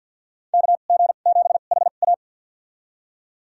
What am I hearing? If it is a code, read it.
Morse code, heard as a keyed tone, about 52 wpm, spaced as for 36 wpm, 700 Hz. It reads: KG7HA
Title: USA Call Signs - 5 Characters - 2x Character Spacing - 50wpm